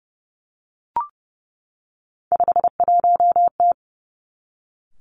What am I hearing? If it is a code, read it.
Morse code, heard as a keyed tone, about 30 wpm, 700 Hz. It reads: 51T
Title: Three Challenging Letter-Number Combinations 30wpm